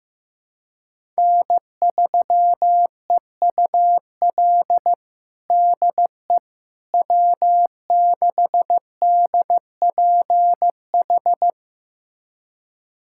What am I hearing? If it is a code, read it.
Morse code, heard as a keyed tone, about 15 wpm, 700 Hz. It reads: N3EUL DE W6DPH
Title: USA Callsign DE Callsign - 5 Characters - 15wpm